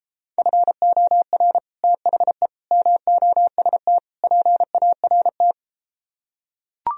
Morse code, heard as a keyed tone, about 33 wpm, 700 Hz. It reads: FOR THE MOST PART